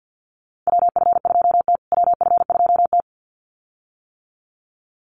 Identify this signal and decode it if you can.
Morse code, heard as a keyed tone, about 50 wpm, 700 Hz. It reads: WF2T WF2T
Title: USA Call Signs Repeated 2x 50wpm